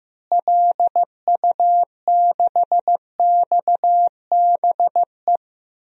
Morse code, heard as a keyed tone, about 15 wpm, 700 Hz. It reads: LU6XBE
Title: DXCC-100 Call Signs - 6 Characters - Repeated 2x - 15wpm